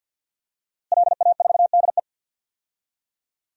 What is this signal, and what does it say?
Morse code, heard as a keyed tone, about 50 wpm, 700 Hz. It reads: LA4DE
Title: DXCC-100 Call Signs - 5 Characters - Repeated 2x - 50wpm